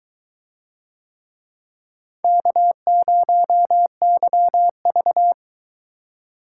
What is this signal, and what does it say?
Morse code, heard as a keyed tone, about 22 wpm, 700 Hz. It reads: K0YV